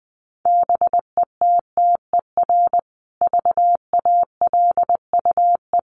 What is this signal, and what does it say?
Morse code, heard as a keyed tone, about 20 wpm, 700 Hz. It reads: BETTER VALUE
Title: Sentences from Top 600 Words 20wpm